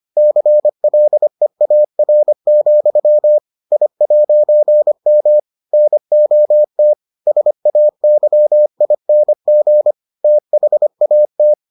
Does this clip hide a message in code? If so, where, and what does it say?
Yes: Morse code from 0.2 to 11.5 s, heard as a keyed tone, about 25 wpm, 600 Hz. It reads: CLEAR, I'M NOT SAYING THAT